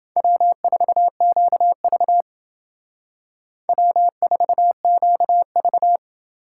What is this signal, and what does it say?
Morse code, heard as a keyed tone, about 28 wpm, 700 Hz. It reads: W4QV W4QV